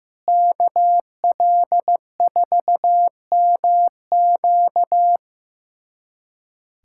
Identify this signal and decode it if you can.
Morse code, heard as a keyed tone, about 15 wpm, 700 Hz. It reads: KL4MQ
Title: USA Callsign DE Callsign - 5 Characters - 15wpm